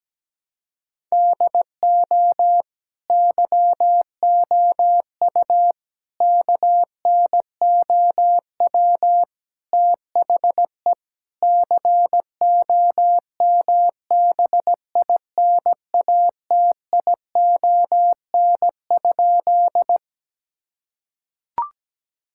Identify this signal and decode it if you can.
Morse code, heard as a keyed tone, about 17 wpm, 700 Hz. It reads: DO YOU KNOW THE COMBINATION?
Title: Sentences from Top 1000 Words - Review Part 6 - 17wpm